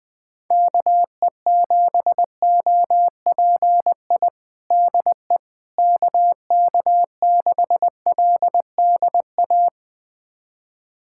Morse code, heard as a keyed tone, about 20 wpm, 700 Hz. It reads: KE7OPI DE KK6LDA
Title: USA Callsign DE Callsign - 6 Characters - 20wpm